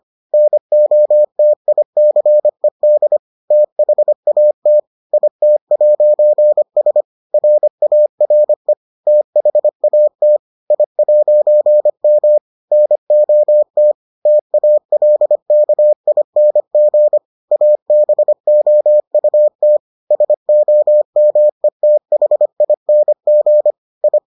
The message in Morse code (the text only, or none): NOTICED THAT IT'S RARE THAT I'M NOT TALKING ABOUT SOMETHING I